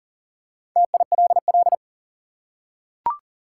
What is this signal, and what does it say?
Morse code, heard as a keyed tone, about 40 wpm, 700 Hz. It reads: TILL
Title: Four Letter Words - 40wpm